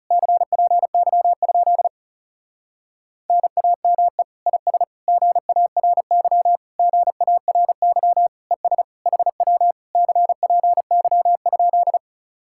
Morse code, heard as a keyed tone, about 40 wpm, 700 Hz. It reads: CPY? NAME IS GARY GARY ES HW CPY?